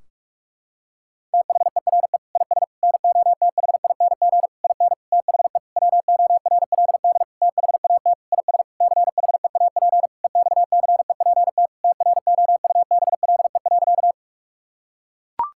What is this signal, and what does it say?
Morse code, heard as a keyed tone, about 41 wpm, 700 Hz. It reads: THERE IS NOTHING IN THE WORLD THAT IS CHEAP EXCEPT TROUBLE.
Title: Sentences from Top 1600 Words 45wpm